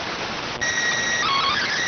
SSTV (slow-scan television), PD 290.